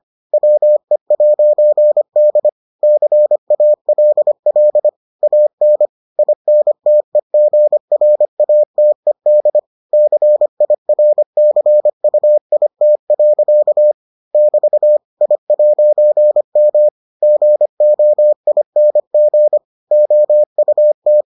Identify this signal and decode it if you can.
Morse code, heard as a keyed tone, about 25 wpm, 600 Hz. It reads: WE'D CALL AN INTEGRATED CIRCUIT. <BT> I'M GOING OUT